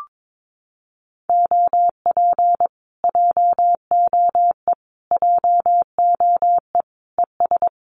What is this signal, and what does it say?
Morse code, heard as a keyed tone, about 22 wpm, 700 Hz. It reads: OP JOE JOE ES
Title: Common QSO Phrases Encore 22wpm